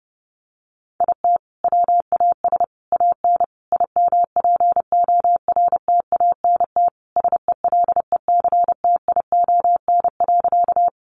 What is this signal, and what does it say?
Morse code, heard as a keyed tone, about 28 wpm, 700 Hz. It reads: IT WAS AN IMPORTANT SELECTION.
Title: Sentences from Top 1400 Words 30wpm